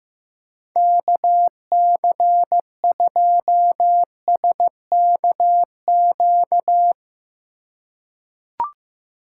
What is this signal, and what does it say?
Morse code, heard as a keyed tone, about 15 wpm, 700 Hz. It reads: KC2SKQ
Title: USA Call Signs - 6 Characters - 2x Character Spacing - 15wpm